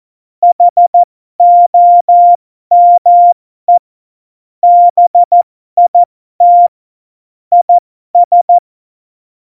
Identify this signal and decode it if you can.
Morse code, heard as a keyed tone, about 14 wpm, spaced as for 12 wpm, 700 Hz. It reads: HOME BIT IS
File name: Morsekurs / Witze